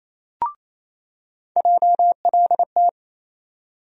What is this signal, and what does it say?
Morse code, heard as a keyed tone, about 28 wpm, 700 Hz. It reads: JLT
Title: USA Call Sign Postfixes - 3 Letters 28wpm